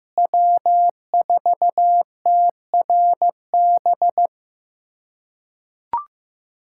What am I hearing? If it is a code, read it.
Morse code, heard as a keyed tone, about 15 wpm, 700 Hz. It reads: W4TRB
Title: USA Callsign DE Callsign - 5 Characters - 15wpm